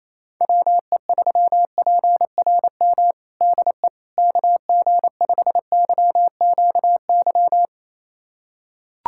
Morse code, heard as a keyed tone, about 28 wpm, 700 Hz. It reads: WE3PRM DE KG5YQY
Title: USA Callsign DE Callsign - 6 Characters - 28wpm